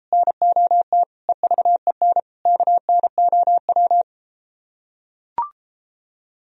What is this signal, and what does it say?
Morse code, heard as a keyed tone, about 33 wpm, 700 Hz. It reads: NOT EVEN KNOW